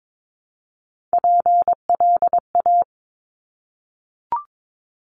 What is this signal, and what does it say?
Morse code, heard as a keyed tone, about 22 wpm, 700 Hz. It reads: PLA
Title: Top Three Letter Combinations in Words 22wpm